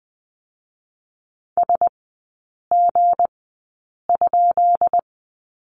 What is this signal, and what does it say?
Morse code, heard as a keyed tone, about 20 wpm, spaced as for 8 wpm, 690 Hz. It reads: SG?
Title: 172 Lesson 28e - Number 9 - Three Characters 20wpm